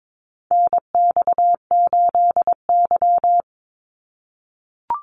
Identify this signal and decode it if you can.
Morse code, heard as a keyed tone, about 22 wpm, 700 Hz. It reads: NX8Y
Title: Call Signs - 22wpm